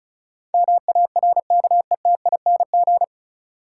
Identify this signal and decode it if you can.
Morse code, heard as a keyed tone, about 36 wpm, 700 Hz. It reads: MARKETING
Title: Sentences from Top 1000 Words - Review Part 5 - New Word Drill - 35wpm